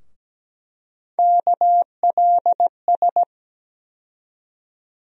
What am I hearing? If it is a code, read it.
Morse code, heard as a keyed tone, about 17 wpm, 700 Hz. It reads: KLS
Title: USA Call Sign Postfixes - 3 Letters 17wpm